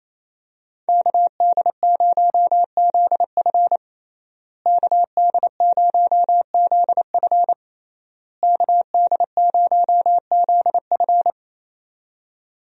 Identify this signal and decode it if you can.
Morse code, heard as a keyed tone, about 27 wpm, 700 Hz. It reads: KD0ZF KD0ZF KD0ZF
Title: USA Call Signs - 5 Characters - Repeated 3x - 28wpm